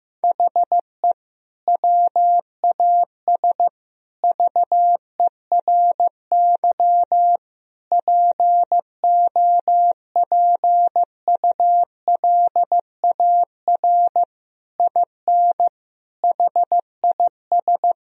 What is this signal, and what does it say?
Morse code, heard as a keyed tone, about 16 wpm, 700 Hz. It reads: HE WAS VERY POPULAR IN HIS
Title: Sentences from Top 1000 Words - Review Part 3 - 15wpm